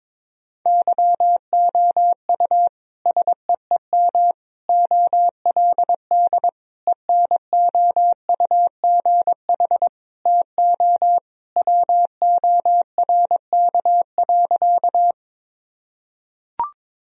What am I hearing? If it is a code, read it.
Morse code, heard as a keyed tone, about 22 wpm, 700 Hz. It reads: YOU SEEM OLD ENOUGH TO WORK.
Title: Sentences from Top 1000 Words - Review Part 1 - 22wpm